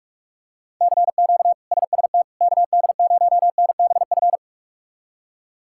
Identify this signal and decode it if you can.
Morse code, heard as a keyed tone, about 46 wpm, 700 Hz. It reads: CQ SST KD0NBF
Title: K1USN SST QSO Practice - 45wpm